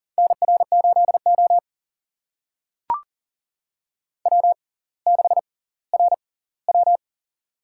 Morse code, heard as a keyed tone, about 40 wpm, spaced as for 20 wpm, 700 Hz. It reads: NR8O W6RW
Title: USA Call Signs - 6x Character Spacing - 40wpm